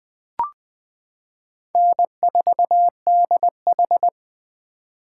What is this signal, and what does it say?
Morse code, heard as a keyed tone, about 20 wpm, 700 Hz. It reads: N4DH
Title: 2023 POTA Hunter Call Signs - 20wpm